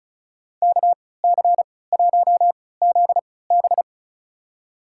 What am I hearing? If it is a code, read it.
Morse code, heard as a keyed tone, about 36 wpm, spaced as for 20 wpm, 700 Hz. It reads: KC1ZB
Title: USA Call Signs - 5 Characters - 3x Character Spacing - 35wpm